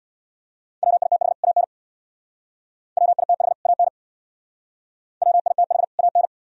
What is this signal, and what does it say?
Morse code, heard as a keyed tone, about 61 wpm, 700 Hz. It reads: WITH AN WITH AN WITH AN
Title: Speed-Racing - Sets of 2 Words 50wpm